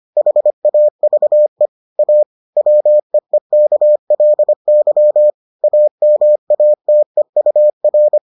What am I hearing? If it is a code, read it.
Morse code, heard as a keyed tone, about 25 wpm, 600 Hz. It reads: HAVE A WEEKLY AMATEUR